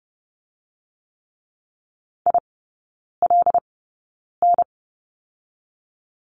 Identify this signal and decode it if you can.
Morse code, heard as a keyed tone, about 30 wpm, 700 Hz. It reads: I L N
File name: 081 Lesson 14e - Number 2 - Three Characters 30wpm